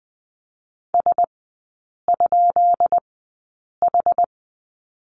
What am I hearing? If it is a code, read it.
Morse code, heard as a keyed tone, about 20 wpm, spaced as for 8 wpm, 700 Hz. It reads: S?H
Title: 133 Lesson 22e - Question Mark - Three Characters 20wpm